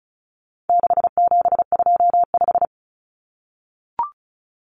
Morse code, heard as a keyed tone, about 34 wpm, 700 Hz. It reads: 6725
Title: Numbers - 4-Digit 35wpm